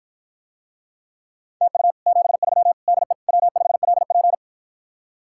Morse code, heard as a keyed tone, about 53 wpm, 700 Hz. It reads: TU 73 DE W5LP